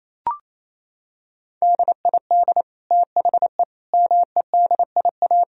Morse code, heard as a keyed tone, about 28 wpm, 700 Hz. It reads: DID THE MEDIA
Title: Sentences from Top 1000 Words - Review Part 4 - 28wpm